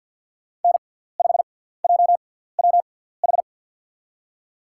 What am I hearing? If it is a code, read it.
Morse code, heard as a keyed tone, about 51 wpm, spaced as for 17 wpm, 700 Hz. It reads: N5JWH
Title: USA Call Signs - 5 Characters - 6x Character Spacing - 50wpm